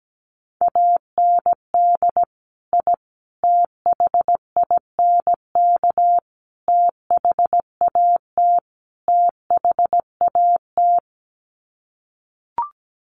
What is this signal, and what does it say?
Morse code, heard as a keyed tone, about 17 wpm, 700 Hz. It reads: AND I THINK THAT THAT
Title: Sets of 5 Words Encore 17wpm